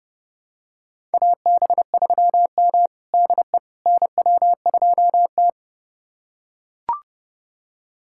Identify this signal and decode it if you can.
Morse code, heard as a keyed tone, about 31 wpm, 700 Hz. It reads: AB3M DE NW2T
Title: USA Callsign DE Callsign - 4 Characters - 30wpm